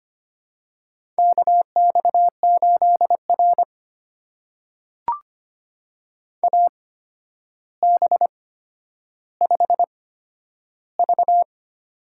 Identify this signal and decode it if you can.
Morse code, heard as a keyed tone, about 25 wpm, spaced as for 11 wpm, 700 Hz. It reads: KX8R AB5V
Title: USA Call Signs - 4 Characters - 8x Character Spacing - 25wpm